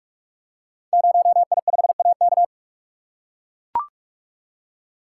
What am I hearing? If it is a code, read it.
Morse code, heard as a keyed tone, about 44 wpm, 690 Hz. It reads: 0I5AK
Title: Code Groups - 5 Letter:Numbers - 45wpm